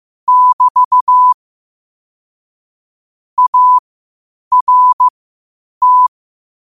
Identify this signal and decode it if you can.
Morse code, heard as a keyed tone, about 15 wpm, spaced as for 8 wpm, 1000 Hz. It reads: <BT> ART
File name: qotdfiles / 8wpm